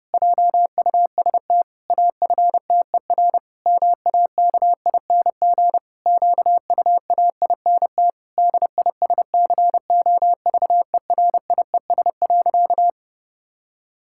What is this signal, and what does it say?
Morse code, heard as a keyed tone, about 32 wpm, 700 Hz. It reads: JUST AFTER MAKING QUAINT DISCOVERIES.
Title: Pangram Challenge v2 - 30wpm